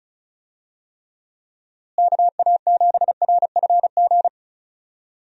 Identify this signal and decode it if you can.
Morse code, heard as a keyed tone, about 36 wpm, 700 Hz. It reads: KA7RFG
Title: USA Call Signs - 6 Characters - Repeated 2x - 35wpm